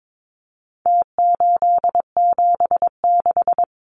Morse code, heard as a keyed tone, about 22 wpm, 700 Hz. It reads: T876
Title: Numbers - Contest Style - 4-Digit 22wpm